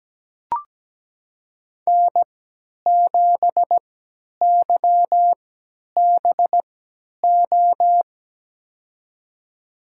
Morse code, heard as a keyed tone, about 17 wpm, spaced as for 10 wpm, 700 Hz. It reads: N7YBO
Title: USA Call Signs - 5 Characters - 3x Character Spacing - 17wpm